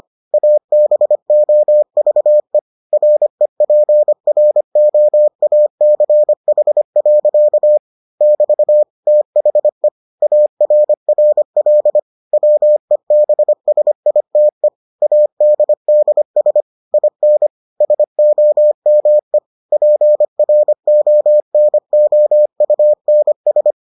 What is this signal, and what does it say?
Morse code, heard as a keyed tone, about 25 wpm, 600 Hz. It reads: ABOVE REPROACH. <BT> THE ARRL WEBSITE ADDS IN SOME PRONOUNS